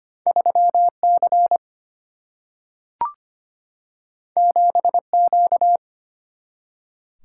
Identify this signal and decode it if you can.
Morse code, heard as a keyed tone, about 25 wpm, 700 Hz. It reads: 3C 7Q
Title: Single-Letter Single-Number 25wpm